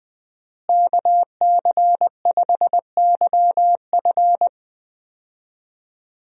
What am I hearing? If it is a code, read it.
Morse code, heard as a keyed tone, about 20 wpm, 700 Hz. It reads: KC5YF
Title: USA Call Signs - 5 Characters - Repeated 2x - 20wpm